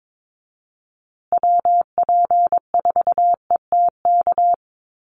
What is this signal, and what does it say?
Morse code, heard as a keyed tone, about 22 wpm, 700 Hz. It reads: WP4ETK
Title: USA Call Signs Repeated 2x 22wpm